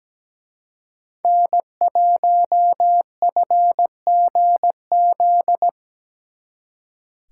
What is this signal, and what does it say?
Morse code, heard as a keyed tone, about 19 wpm, 700 Hz. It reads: N1FGZ
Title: Speed Racing - USA Call Signs 17wpm